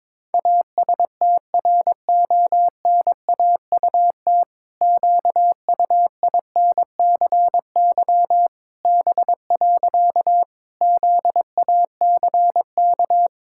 Morse code, heard as a keyed tone, about 22 wpm, 700 Hz. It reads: ASTRONAUT QUINCY B. ZACK
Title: Pangram Challenge v2 - 22wpm